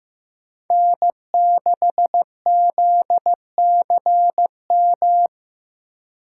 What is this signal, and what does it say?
Morse code, heard as a keyed tone, about 15 wpm, 700 Hz. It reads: N6ZCM
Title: USA Call Signs - 5 Characters - 3x Character Spacing - 15wpm